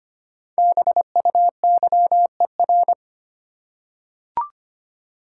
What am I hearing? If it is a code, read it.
Morse code, heard as a keyed tone, about 25 wpm, 700 Hz. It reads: BUYER
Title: Five Letter Words - 25wpm